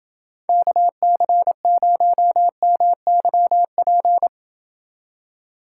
Morse code, heard as a keyed tone, about 29 wpm, 700 Hz. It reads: KC0MYP